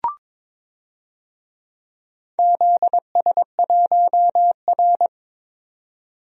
Morse code, heard as a keyed tone, about 22 wpm, 700 Hz. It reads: ZS1R
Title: DXCC-100 Call Signs - 4 Characters - 22wpm